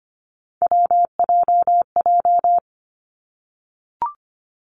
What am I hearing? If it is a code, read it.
Morse code, heard as a keyed tone, about 25 wpm, 700 Hz. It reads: WJJ
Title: Three Challenging Letter-Number Combinations 25wpm